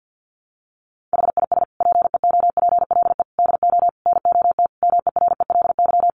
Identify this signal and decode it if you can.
Morse code, heard as a keyed tone, about 50 wpm, 700 Hz. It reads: HIS PEOPLE DO NOT MERELY
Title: Sentences from Top 1400 Words 50wpm